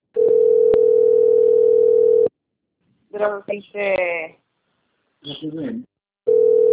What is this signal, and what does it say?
Ringback tone (North America), 440 + 480 Hz, with speech.
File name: rg-1600-unknown- - - .33742